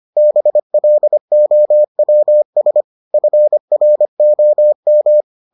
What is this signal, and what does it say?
Morse code, heard as a keyed tone, about 25 wpm, 600 Hz. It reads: BLOWS FROM